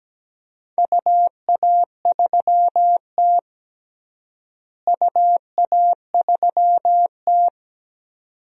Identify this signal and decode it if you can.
Morse code, heard as a keyed tone, about 17 wpm, 700 Hz. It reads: UA3T UA3T
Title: DXCC-100 Call Signs - 4 Characters - Repeated 2x - 17wpm